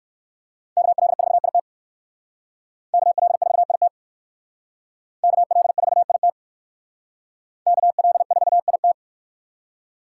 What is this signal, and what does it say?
Morse code, heard as a keyed tone, about 56 wpm, 700 Hz. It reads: KL4IT KL4IT KL4IT KL4IT